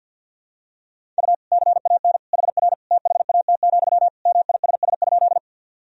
Morse code, heard as a keyed tone, about 52 wpm, 700 Hz. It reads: U CAN HR THAT, MISS?
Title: Sentences from Top 500 Words and Abbreviations 50wpm